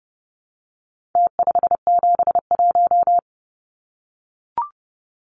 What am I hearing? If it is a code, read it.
Morse code, heard as a keyed tone, about 30 wpm, 700 Hz. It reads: T571
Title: Numbers - Contest Style - 4-Digit 30wpm